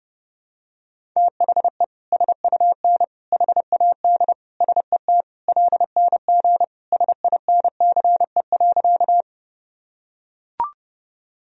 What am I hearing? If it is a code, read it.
Morse code, heard as a keyed tone, about 30 wpm, 700 Hz. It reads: THE SUN HAD SET LNG SINCE.
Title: Sentences from Top 300 Words and Abbreviations 30wpm